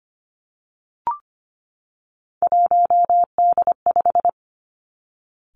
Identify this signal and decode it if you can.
Morse code, heard as a keyed tone, about 25 wpm, 700 Hz. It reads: 1D5
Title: Three Challenging Letter-Number Combinations 25wpm